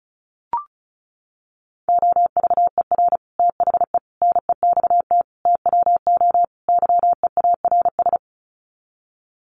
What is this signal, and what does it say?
Morse code, heard as a keyed tone, about 34 wpm, 700 Hz. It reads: OVER THE NEXT TWO YEARS
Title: Sets of 5 Words Encore2 35wpm